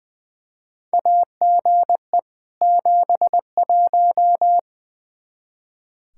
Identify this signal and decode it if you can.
Morse code, heard as a keyed tone, about 20 wpm, 700 Hz. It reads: AGE 71
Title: Practice QSO - Exchange 3 - More Info - 20wpm